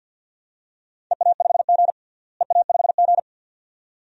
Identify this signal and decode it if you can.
Morse code, heard as a keyed tone, about 50 wpm, 700 Hz. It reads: EA5G EA5G
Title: DXCC-100 Call Signs - 4 Characters - Repeated 2x - 50wpm